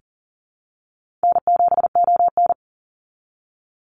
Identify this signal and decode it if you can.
Morse code, heard as a keyed tone, about 40 wpm, 700 Hz. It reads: N7ON
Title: Call Signs Repeated 3x - 40wpm